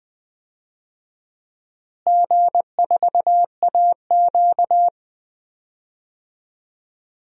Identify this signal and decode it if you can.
Morse code, heard as a keyed tone, about 20 wpm, 700 Hz. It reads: G4AQ
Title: DXCC-100 Call Signs - 4 Characters - Repeated 3x - 20wpm